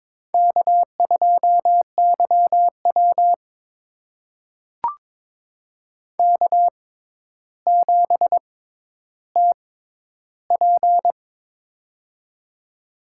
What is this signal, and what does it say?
Morse code, heard as a keyed tone, about 22 wpm, spaced as for 11 wpm, 700 Hz. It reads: K2YW K7TP
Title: USA Call Signs - 4 Characters - 6x Character Spacing - 22wpm